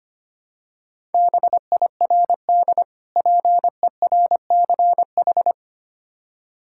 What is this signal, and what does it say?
Morse code, heard as a keyed tone, about 25 wpm, 700 Hz. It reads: BIRD PERCH
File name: practice / 25wpm / Difficult Words with Hints - 25wpm